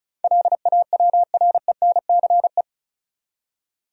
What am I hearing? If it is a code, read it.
Morse code, heard as a keyed tone, about 36 wpm, 700 Hz. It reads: LAWRENCE
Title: USA Common Names - 35wpm